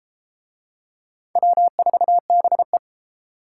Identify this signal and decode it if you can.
Morse code, heard as a keyed tone, about 29 wpm, 700 Hz. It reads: W4BE